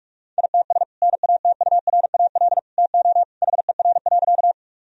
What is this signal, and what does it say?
Morse code, heard as a keyed tone, about 45 wpm, 700 Hz. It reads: ITS NATURAL TO HER.
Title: Sentences from Top 1000 Words - Review Part 4 - 45wpm